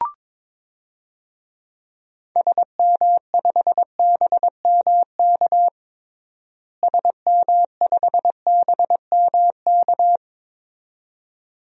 Morse code, heard as a keyed tone, about 22 wpm, 700 Hz. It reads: SM5BMK SM5BMK
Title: DXCC-100 Call Signs - 6 Characters - Repeated 2x - 22wpm